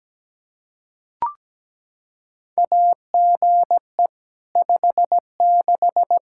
Morse code, heard as a keyed tone, about 17 wpm, 700 Hz. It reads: AGE 56
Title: K1USN SST QSO Practice 17wpm